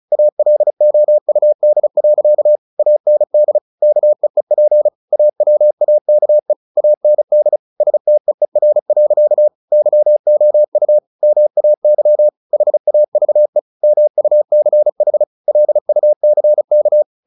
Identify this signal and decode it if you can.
Morse code, heard as a keyed tone, about 34 wpm, 600 Hz. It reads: ALOUD. AND KEEP AWAKE AND STEER. YOU MAY HAVE MUCH LUCK
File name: the old man and the sea 2 35wpm